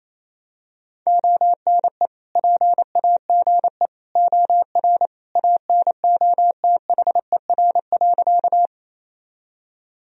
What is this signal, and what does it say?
Morse code, heard as a keyed tone, about 27 wpm, 700 Hz. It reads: ONE PAGE OR ANOTHER.
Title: Sentences from Top 1000 Words - Review Part 1 - 28wpm